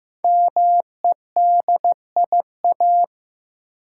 Morse code, heard as a keyed tone, about 15 wpm, 700 Hz. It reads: MEDIA
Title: Sentences from Top 1000 Words - Review Part 4 - New Word Drill - 15wpm